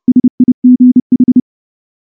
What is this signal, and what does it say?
Morse code, heard as a keyed tone, about 30 wpm, 250 Hz. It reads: SIGH
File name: morse-training-30WPM